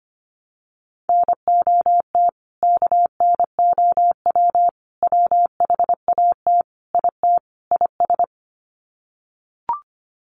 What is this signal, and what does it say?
Morse code, heard as a keyed tone, about 25 wpm, 700 Hz. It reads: NOT KNOW WHAT IT IS
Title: Sets of 5 Words Encore 25wpm